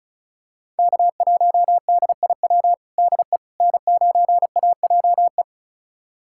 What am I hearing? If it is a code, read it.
Morse code, heard as a keyed tone, about 36 wpm, 700 Hz. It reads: K1DIW DE N9AJE
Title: USA Callsign DE Callsign - 5 Characters - 35wpm